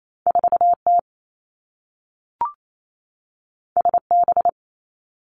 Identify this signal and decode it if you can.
Morse code, heard as a keyed tone, about 28 wpm, 700 Hz. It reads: 4 T S B
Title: Two Challenging Letter-Number Combinations 28wpm